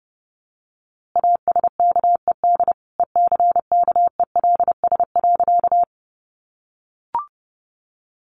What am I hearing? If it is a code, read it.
Morse code, heard as a keyed tone, about 30 wpm, 700 Hz. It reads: ASKED ECKELS.
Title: A Sound Of Thunder 30wpm